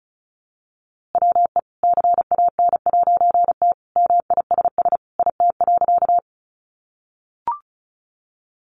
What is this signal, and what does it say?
Morse code, heard as a keyed tone, about 34 wpm, 700 Hz. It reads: WE CAN'T MISS IT.
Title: A Sound Of Thunder 35wpm